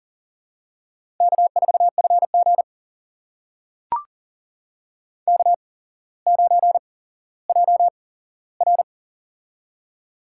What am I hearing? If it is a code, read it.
Morse code, heard as a keyed tone, about 40 wpm, spaced as for 17 wpm, 700 Hz. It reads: K4FG K9JR
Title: USA Call Signs - 4 Characters - 8x Character Spacing - 40wpm